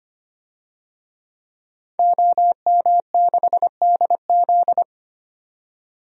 Morse code, heard as a keyed tone, about 25 wpm, 700 Hz. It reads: OM6DZ
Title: DXCC-100 Call Signs - 5 Characters - 25wpm